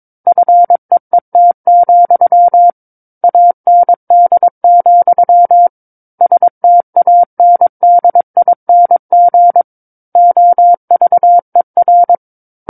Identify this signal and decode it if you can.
Morse code, heard as a keyed tone, about 22 wpm, 700 Hz. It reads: FEET, AND, STANDING OVER
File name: Book One / 13-Chapter 13 - 22 WPM